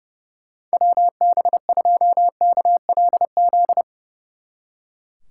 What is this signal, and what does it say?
Morse code, heard as a keyed tone, about 36 wpm, 700 Hz. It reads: WB2KLZ
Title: Speed Racing - USA 2x3 Call Signs 30wpm